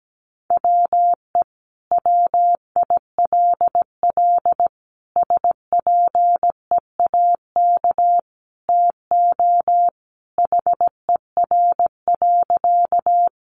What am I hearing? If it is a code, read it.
Morse code, heard as a keyed tone, about 17 wpm, 700 Hz. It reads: WE WILL SPEAK TO HER.
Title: Sentences from Top 1100 Words 17wpm